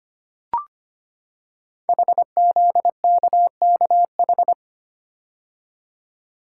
Morse code, heard as a keyed tone, about 25 wpm, 700 Hz. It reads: HZKKH
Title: Code Groups - 5 Letters - 25wpm